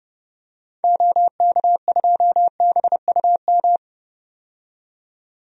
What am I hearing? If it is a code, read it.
Morse code, heard as a keyed tone, about 32 wpm, 700 Hz. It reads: OK2BUM
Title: DXCC-100 Call Signs - 6 Characters - 30wpm